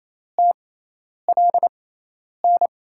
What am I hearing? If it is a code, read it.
Morse code, heard as a keyed tone, about 28 wpm, spaced as for 10 wpm, 700 Hz. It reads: TLN
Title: USA Call Signs - 5 Characters - 6x Character Spacing - 28wpm